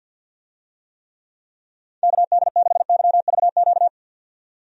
Morse code, heard as a keyed tone, about 51 wpm, 700 Hz. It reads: KD6YVQ
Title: USA Call Signs - 6 Characters - 50wpm